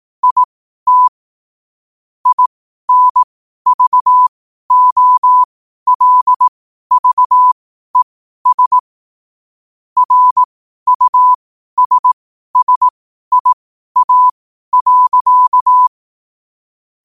Morse code, heard as a keyed tone, about 18 wpm, spaced as for 12 wpm, 1000 Hz. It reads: IT INVOLVES RUSSIA.
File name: qotdfiles / 12wpm